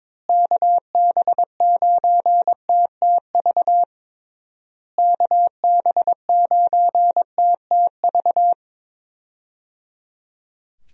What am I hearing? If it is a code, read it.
Morse code, heard as a keyed tone, about 22 wpm, 700 Hz. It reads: KB9TTV KB9TTV
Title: USA Call Signs - 6 Characters - Repeated 3x - 22wpm